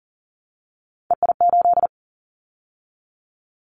Morse code, heard as a keyed tone, about 40 wpm, 700 Hz. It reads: EI8
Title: Callsign Prefixes for Top 50 DXCC - 40wpm